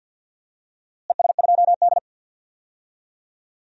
Morse code, heard as a keyed tone, about 53 wpm, 700 Hz. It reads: ES2D
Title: DXCC-100 Call Signs - 4 Characters - Repeated 3x - 50wpm